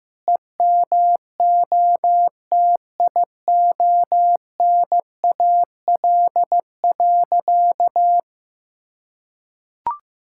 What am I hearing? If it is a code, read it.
Morse code, heard as a keyed tone, about 15 wpm, 700 Hz. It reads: EMOTIONAL.
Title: Sentences from Top 1000 Words - Review Part 5 - 15wpm